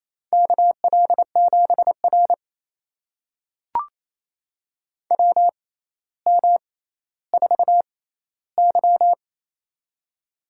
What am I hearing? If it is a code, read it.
Morse code, heard as a keyed tone, about 29 wpm, spaced as for 15 wpm, 700 Hz. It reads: KL7R WM4Y
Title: USA Call Signs - 4 Characters - 6x Character Spacing - 28wpm